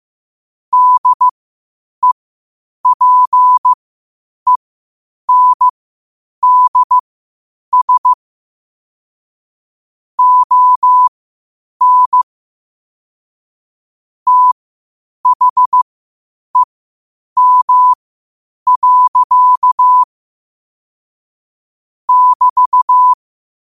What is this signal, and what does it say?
Morse code, heard as a keyed tone, about 15 wpm, spaced as for 8 wpm, 1000 Hz. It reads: DEPENDS ON THEM. <BT>
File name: qotdfiles / 8wpm